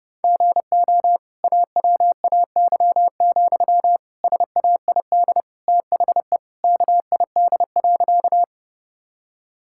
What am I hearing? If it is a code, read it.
Morse code, heard as a keyed tone, about 32 wpm, 700 Hz. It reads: GO AWAY, SAID THE KID.
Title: Sentences from Top 1600 Words 30wpm